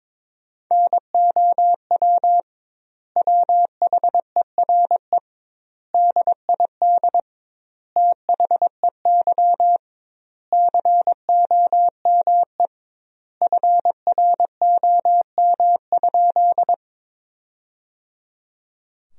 Morse code, heard as a keyed tone, about 22 wpm, 700 Hz. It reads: NOW WHERE DID THEY COME FROM?
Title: 2x Word-Spacing - Sentences from Top 200 Words - 22wpm